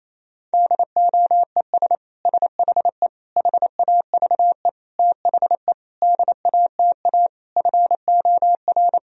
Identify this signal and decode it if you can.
Morse code, heard as a keyed tone, about 28 wpm, 700 Hz. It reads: DOES SHE HAVE THE DATA FOR
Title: Sentences from Top 1000 Words - Review Part 3 - 28wpm